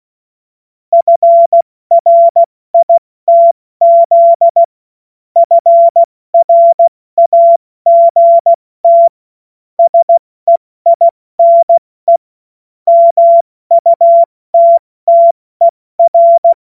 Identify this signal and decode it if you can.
Morse code, heard as a keyed tone, about 16 wpm, spaced as for 14 wpm, 680 Hz. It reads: FRITZ FRAGT SEINE MUTTER
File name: Morsekurs / Witze / Witz 07. September 2021, 70 F80